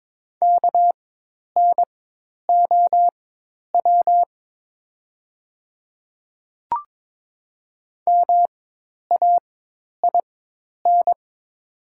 Morse code, heard as a keyed tone, about 22 wpm, spaced as for 10 wpm, 700 Hz. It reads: KNOW MAIN
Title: 4x Character Spacing - Top 300 Words - 22wpm